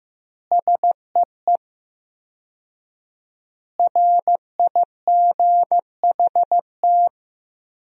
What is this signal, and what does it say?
Morse code, heard as a keyed tone, about 15 wpm, 700 Hz. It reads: SEE RIGHT
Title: 4x Word-Spacing - Sentences from Top 200 Words - 15wpm